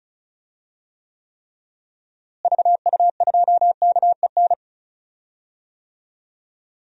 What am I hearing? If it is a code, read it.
Morse code, heard as a keyed tone, about 34 wpm, 700 Hz. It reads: VU2KEN
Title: DXCC-100 Call Signs - 6 Characters - Repeated 3x - 35wpm